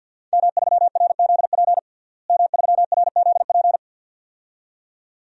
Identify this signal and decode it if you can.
Morse code, heard as a keyed tone, about 50 wpm, 700 Hz. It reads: M3RZP M3RZP
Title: DXCC-100 Call Signs - 5 Characters - Repeated 2x - 50wpm